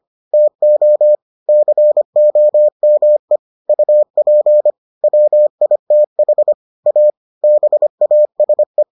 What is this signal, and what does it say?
Morse code, heard as a keyed tone, about 25 wpm, 600 Hz. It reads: TO COME UP WITH A BASE